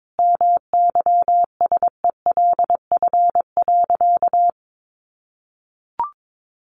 Morse code, heard as a keyed tone, about 22 wpm, 700 Hz. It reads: MYSELF.
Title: Sentences from Top 1300 Words 22wpm